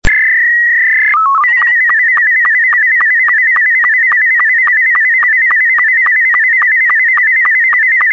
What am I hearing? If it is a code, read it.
SSTV (slow-scan television), Scottie 2.
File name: sstv / sstv1 / scottie s2